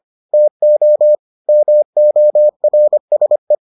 Morse code, heard as a keyed tone, about 25 wpm, 600 Hz. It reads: TO MORSE